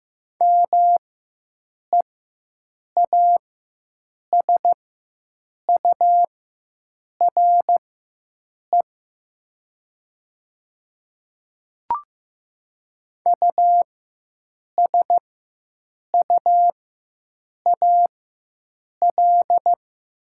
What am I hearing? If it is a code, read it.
Morse code, heard as a keyed tone, about 15 wpm, spaced as for 7 wpm, 700 Hz. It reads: MEASURE USUAL
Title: 4x Character Spacing - Top 500 Words - 15wpm